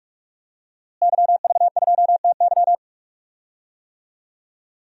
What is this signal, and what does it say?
Morse code, heard as a keyed tone, about 43 wpm, 700 Hz. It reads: YV2TY
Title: DXCC-100 Call Signs - 5 Characters - Repeated 3x - 45wpm